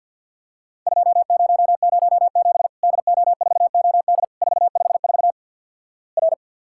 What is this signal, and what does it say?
Morse code, heard as a keyed tone, about 48 wpm, 700 Hz. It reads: 2007 DO4OD 454 R
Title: Worked All Europe DX Contest - QTC Exchange with Repeats - 50wpm